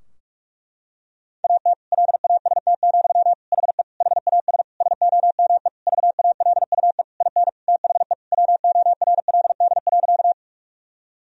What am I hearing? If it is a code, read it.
Morse code, heard as a keyed tone, about 47 wpm, 700 Hz. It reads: AT LAST, HE HAS SOME VALUE IN THE WORLD.
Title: Sentences from Top 1000 Words - Review Part 3 - 45wpm